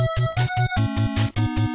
There is no code.